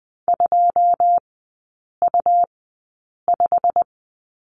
Morse code, heard as a keyed tone, about 20 wpm, spaced as for 8 wpm, 700 Hz. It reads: 2U5
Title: 166 Lesson 27e - Number 7 - Three Characters 20wpm